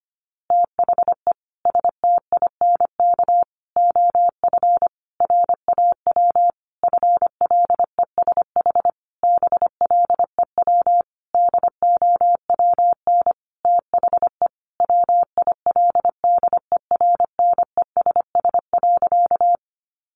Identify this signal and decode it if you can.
Morse code, heard as a keyed tone, about 25 wpm, 700 Hz. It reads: THE STINK OF RAW FLESH BLEW DOWN THE WILDERNESS.
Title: A Sound Of Thunder 25wpm